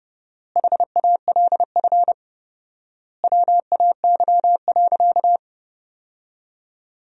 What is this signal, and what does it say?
Morse code, heard as a keyed tone, about 32 wpm, 700 Hz. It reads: HALF WAY.
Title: 4x Word-Spacing - Sentences from Top 400 Words - 30wpm